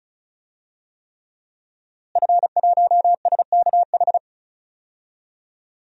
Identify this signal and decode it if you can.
Morse code, heard as a keyed tone, about 36 wpm, 700 Hz. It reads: F1SKH
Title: DXCC-100 Call Signs - 5 Characters - Repeated 3x - 35wpm